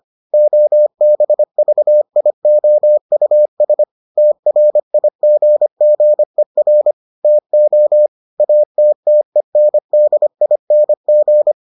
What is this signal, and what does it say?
Morse code, heard as a keyed tone, about 25 wpm, 600 Hz. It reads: OBVIOUS TRIGGER TO ATTENDING